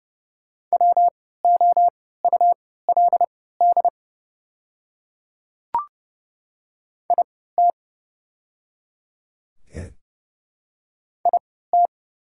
Morse code, heard as a keyed tone, about 30 wpm, spaced as for 17 wpm, 700 Hz. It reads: WOULD IT IT
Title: 3x Character Spacing - Top 100 Words - 30wpm